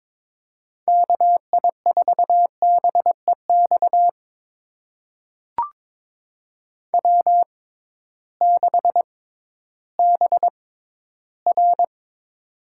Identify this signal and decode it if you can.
Morse code, heard as a keyed tone, about 22 wpm, spaced as for 13 wpm, 700 Hz. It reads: KI4BEX W6BR
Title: USA Call Signs - 6x Character Spacing - 22wpm